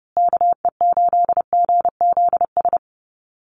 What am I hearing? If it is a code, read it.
Morse code, heard as a keyed tone, about 29 wpm, 700 Hz. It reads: KE8GZS
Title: USA Call Signs Repeated 3x 30wpm